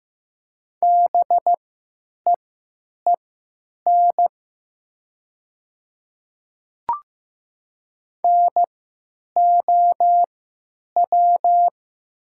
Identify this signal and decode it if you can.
Morse code, heard as a keyed tone, about 15 wpm, spaced as for 9 wpm, 700 Hz. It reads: BEEN NOW
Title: 3x Character Spacing - Top 100 Words - 15wpm